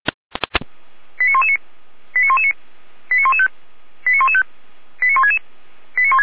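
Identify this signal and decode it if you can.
Five-tone selective call (ZVEI), heard as quick runs of five tones, tones of about 70 ms. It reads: 89119 89119 89115 89115 89166
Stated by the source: ZVEI five-tone selective call received by SDR.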